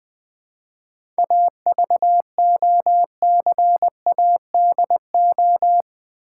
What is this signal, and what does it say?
Morse code, heard as a keyed tone, about 20 wpm, 700 Hz. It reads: AVOCADO
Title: Difficult Words with Hints - 20wpm